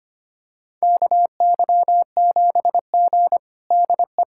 Morse code, heard as a keyed tone, about 25 wpm, 700 Hz. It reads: KY7G DE
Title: USA Callsign DE Callsign - 4 Characters - 25wpm